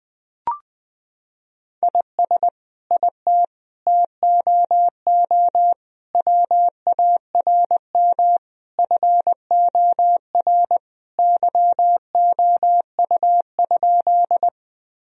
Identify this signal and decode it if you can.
Morse code, heard as a keyed tone, about 20 wpm, 700 Hz. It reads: IS IT TOO WARM FOR YOU?
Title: Sentences from Top 1000 Words - Review Part 2 - 20wpm